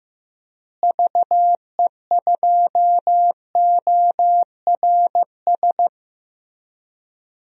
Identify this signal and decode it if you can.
Morse code, heard as a keyed tone, about 15 wpm, 700 Hz. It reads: VE2ORS
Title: DXCC-100 Call Signs - 6 Characters - Repeated 3x - 15wpm